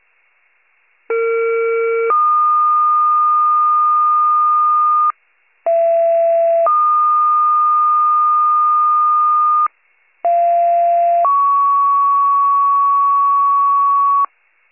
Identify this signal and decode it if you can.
Two-tone page, heard as a one-second tone, then a three-second tone, about 460 then 1220 Hz, about 670 then 1220 Hz, about 690 then 1120 Hz. It is heard from 1.1 to 14.2 s.
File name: wcalert